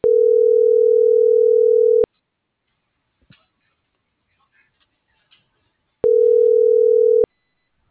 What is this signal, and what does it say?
Ringback tone (North America), 440 + 480 Hz.
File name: CENTRAL NAVIERAS / out oscar.monta.o